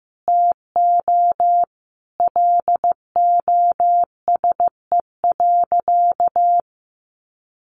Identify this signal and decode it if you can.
Morse code, heard as a keyed tone, about 15 wpm, 700 Hz. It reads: TO LOSE.
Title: Sentences from Top 1100 Words 15wpm